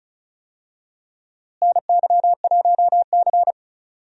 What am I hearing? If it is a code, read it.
Morse code, heard as a keyed tone, about 34 wpm, 700 Hz. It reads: NY1C
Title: USA Call Signs - 4 Characters - Repeated 2x - 35wpm